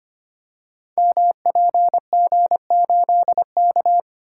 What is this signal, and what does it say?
Morse code, heard as a keyed tone, about 25 wpm, 700 Hz. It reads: MPG8K
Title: Code Groups - 5 Letter:Numbers - 25wpm